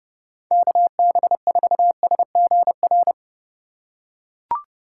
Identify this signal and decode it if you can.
Morse code, heard as a keyed tone, about 30 wpm, 700 Hz. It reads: KB4SGR
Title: USA Call Signs - 3x Character Spacing - 30wpm